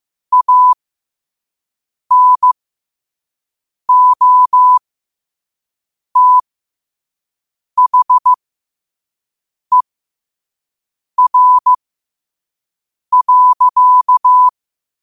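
Morse code, heard as a keyed tone, about 15 wpm, spaced as for 5 wpm, 1000 Hz. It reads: ANOTHER.
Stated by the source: Morse code practice at 5 wpm.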